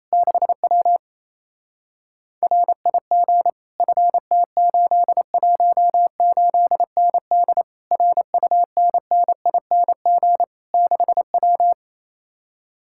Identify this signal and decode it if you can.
Morse code, heard as a keyed tone, about 33 wpm, 700 Hz. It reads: 6W RIG FT818ND RUNNING 6W
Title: Speed-Racing Common QSO Phrases - 28wpm